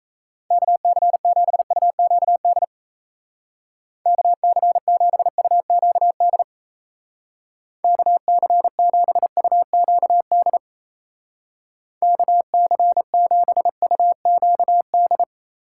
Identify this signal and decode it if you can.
Morse code, heard as a keyed tone, about 33 wpm, 700 Hz. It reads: KC7UQD KC7UQD KC7UQD KC7UQD
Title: Speed Racing - USA 2x3 Call Signs 28wpm